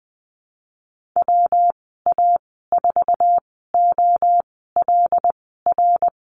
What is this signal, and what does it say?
Morse code, heard as a keyed tone, about 20 wpm, spaced as for 15 wpm, 700 Hz. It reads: WA4OLR
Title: 140 Lesson 23g - Letter f - Call Signs 20wpm